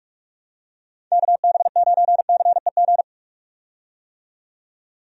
Morse code, heard as a keyed tone, about 43 wpm, 700 Hz. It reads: KB9CEG
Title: USA Call Signs - 6 Characters - 45wpm